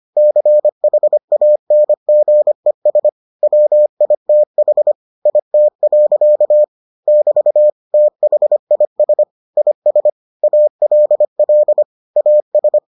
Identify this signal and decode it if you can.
Morse code, heard as a keyed tone, about 25 wpm, 600 Hz. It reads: CHANGES WITH IT. <BT> THIS IS ALL AS